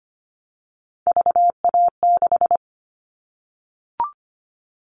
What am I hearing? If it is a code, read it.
Morse code, heard as a keyed tone, about 25 wpm, 700 Hz. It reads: VA6
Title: Callsign Prefixes for Top 50 DXCC - 25wpm